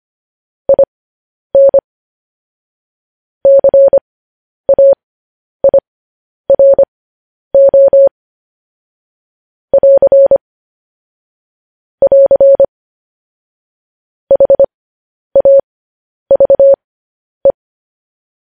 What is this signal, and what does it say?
Morse code, heard as a keyed tone, about 25 wpm, spaced as for 10 wpm, 550 Hz. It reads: IN CAIRO <AR> <AR> HAVE